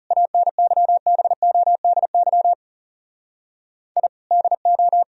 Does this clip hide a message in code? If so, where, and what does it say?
Yes: Morse code from 0.1 to 5.0 s, heard as a keyed tone, about 40 wpm, 700 Hz. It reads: ANYBODY I DO